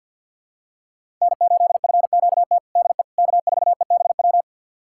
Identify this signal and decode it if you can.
Morse code, heard as a keyed tone, about 49 wpm, 700 Hz. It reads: N8FQT DE K4EBW
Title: USA Callsign DE Callsign - 5 Characters - 50wpm